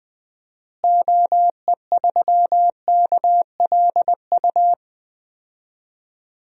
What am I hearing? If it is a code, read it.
Morse code, heard as a keyed tone, about 20 wpm, 700 Hz. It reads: OE3KLU
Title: DXCC-100 Call Signs - 6 Characters - Repeated 3x - 20wpm